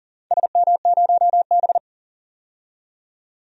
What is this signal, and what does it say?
Morse code, heard as a keyed tone, about 39 wpm, 700 Hz. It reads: SM0B
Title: DXCC-100 Call Signs - 4 Characters - 40wpm